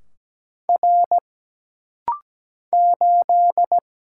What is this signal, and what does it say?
Morse code, heard as a keyed tone, about 17 wpm, 700 Hz. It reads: R 8
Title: Single Letter-Number - Rapid-Fire 17wpm